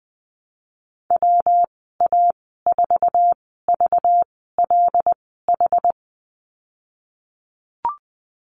Morse code, heard as a keyed tone, about 19 wpm, spaced as for 14 wpm, 700 Hz. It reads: WA4VLH
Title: 193 Lesson 31g - Letter v - Call Signs 20wpm